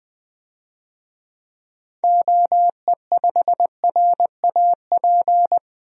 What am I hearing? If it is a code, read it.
Morse code, heard as a keyed tone, about 20 wpm, 700 Hz. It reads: OE5RAP
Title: DXCC-100 Call Signs - 6 Characters - 20wpm